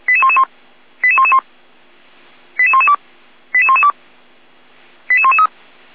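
Five-tone selective call (ZVEI), heard as quick runs of five tones, tones of about 70 ms. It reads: 88111 88112 88112 88113
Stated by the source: ZVEI five-tone call received on an SDR.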